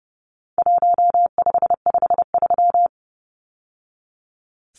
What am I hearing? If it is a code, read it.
Morse code, heard as a keyed tone, about 30 wpm, 700 Hz. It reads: 1553
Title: Numbers - 4-Digit 30wpm